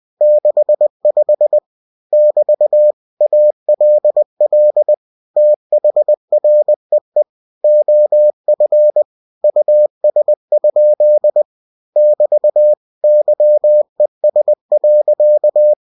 Morse code, heard as a keyed tone, about 20 wpm, 600 Hz. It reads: 65 <BT> ALL THREE OF US? <BT> YES.
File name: cw-books / winnie-the-pooh / winnie-20wpm-page-0065